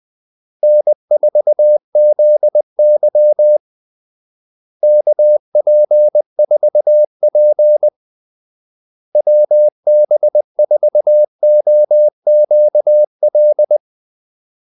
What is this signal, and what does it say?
Morse code, heard as a keyed tone, about 20 wpm, 600 Hz. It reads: N4ZY KP4P WB4OQL
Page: Kaldesignaler 20 wpm